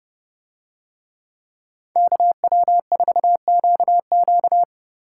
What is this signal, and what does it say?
Morse code, heard as a keyed tone, about 27 wpm, 700 Hz. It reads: KW4QQ